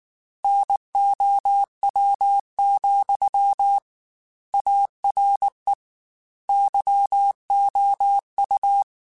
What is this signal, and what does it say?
Morse code, heard as a keyed tone, about 19 wpm, 800 Hz. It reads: NOW, ARE YOU
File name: Technical Subjects / Technical Subjects-mp3s / OR Gate @18